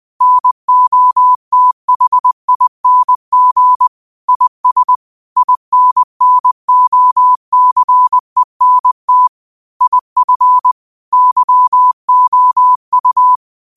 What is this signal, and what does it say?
Morse code, heard as a keyed tone, about 20 wpm, 1000 Hz. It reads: NOTHING IS INNOCENT IF YOU